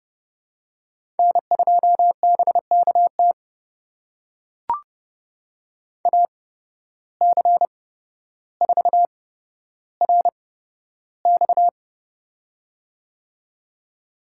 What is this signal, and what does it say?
Morse code, heard as a keyed tone, about 30 wpm, spaced as for 13 wpm, 700 Hz. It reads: N2BKT AC4RX
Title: USA Call Signs - 5 Characters - 8x Character Spacing - 30wpm